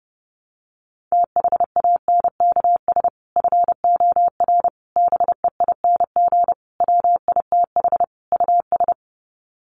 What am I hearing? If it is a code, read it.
Morse code, heard as a keyed tone, about 28 wpm, 700 Hz. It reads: THANKS FOR BEING WITH US
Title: Sets of 5 Words 30wpm